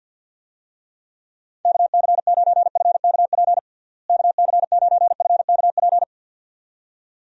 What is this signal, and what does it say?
Morse code, heard as a keyed tone, about 51 wpm, 700 Hz. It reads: KC9FKP KC9FKP
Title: USA Call Signs - 6 Characters - Repeated 2x - 50wpm